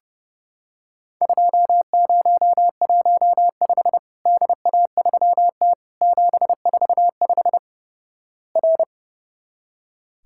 Morse code, heard as a keyed tone, about 30 wpm, 700 Hz. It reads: 2015 DA3T 745 R
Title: Worked All Europe DX Contest - QTC Exchange - 30wpm